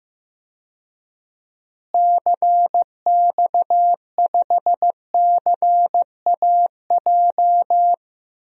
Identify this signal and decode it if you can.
Morse code, heard as a keyed tone, about 15 wpm, 700 Hz. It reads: CX5CAJ
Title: DXCC-100 Call Signs - 6 Characters - Repeated 2x - 15wpm